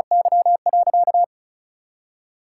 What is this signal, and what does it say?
Morse code, heard as a keyed tone, about 36 wpm, 700 Hz. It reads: Y.
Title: Sentences from Top 1000 Words - Review Part 5 - 35wpm